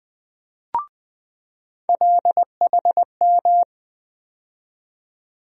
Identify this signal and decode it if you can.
Morse code, heard as a keyed tone, about 20 wpm, 700 Hz. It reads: LHM
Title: USA Call Sign Postfixes - 3 Letters 20wpm